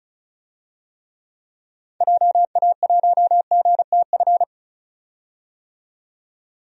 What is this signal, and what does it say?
Morse code, heard as a keyed tone, about 36 wpm, 700 Hz. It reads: JA1GTF
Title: DXCC-100 Call Signs - 6 Characters - 35wpm